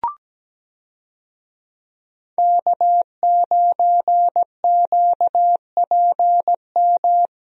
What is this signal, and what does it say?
Morse code, heard as a keyed tone, about 17 wpm, 700 Hz. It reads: K9QPM
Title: USA Call Signs - 5 Characters - Repeated 2x - 17wpm